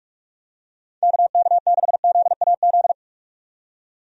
Morse code, heard as a keyed tone, about 43 wpm, 700 Hz. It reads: KK6ZAZ
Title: USA Call Signs - 6 Characters - 45wpm